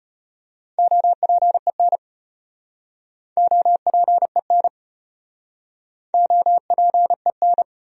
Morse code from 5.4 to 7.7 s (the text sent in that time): OPEN